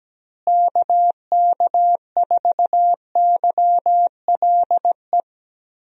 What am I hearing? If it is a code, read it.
Morse code, heard as a keyed tone, about 17 wpm, 700 Hz. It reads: KK4YLE
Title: USA Call Signs - 6 Characters - Repeated 2x - 17wpm